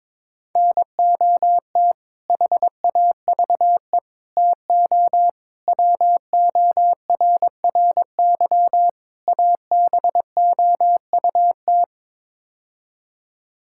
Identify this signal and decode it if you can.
Morse code, heard as a keyed tone, about 22 wpm, 700 Hz. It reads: NOT HAVE TO WORRY ABOUT
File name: practice / 22wpm / Build A Phrase - 5-Word Phrases - 22wpm